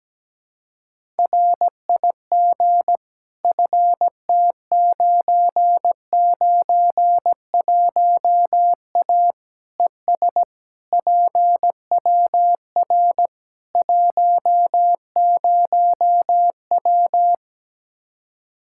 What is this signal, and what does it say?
Morse code, heard as a keyed tone, about 17 wpm, 700 Hz. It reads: RIG FT991A ES PWR 10W
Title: Practice QSO - Exchange 2 - Info - 17wpm